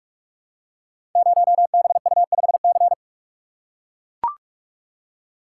Morse code, heard as a keyed tone, about 44 wpm, 690 Hz. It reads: 0BU5C
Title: Code Groups - 5 Letter:Numbers - 45wpm